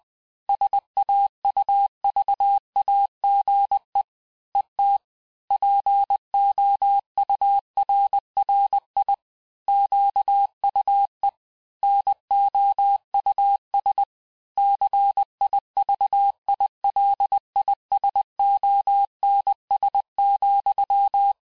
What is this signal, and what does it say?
Morse code, heard as a keyed tone, about 20 wpm, 800 Hz. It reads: SAUVAGE ET POURRI QUE NOUS CIVILISONS,
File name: cours-du-mercredi / audios / cours-858-texte-clais-vitesse-19mpm